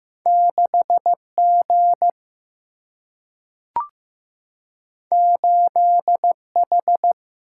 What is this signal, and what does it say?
Morse code, heard as a keyed tone, about 15 wpm, 700 Hz. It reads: 6 G 8 H
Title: Single-Letter Single-Number 15wpm